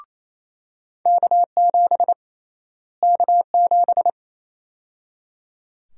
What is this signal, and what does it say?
Morse code, heard as a keyed tone, about 28 wpm, 700 Hz. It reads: K7 K7
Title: USA Call Sign Prefixes - Repeated 2x - 28wpm